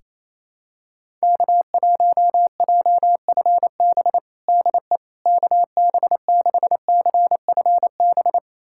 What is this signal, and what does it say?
Morse code, heard as a keyed tone, about 28 wpm, 700 Hz. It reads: K1JFB DE KB6CFB
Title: USA Callsign DE Callsign 28wpm